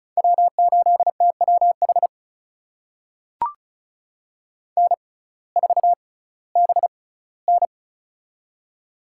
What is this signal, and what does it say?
Morse code, heard as a keyed tone, about 36 wpm, spaced as for 19 wpm, 700 Hz. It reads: W8TWH N4BN
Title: USA Call Signs - 6x Character Spacing - 35wpm